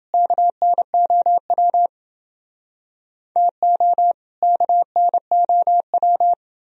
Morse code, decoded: KNOW TO KNOW

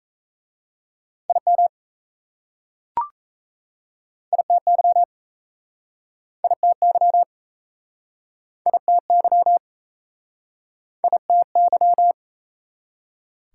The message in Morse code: IM ITY ITY ITY ITY

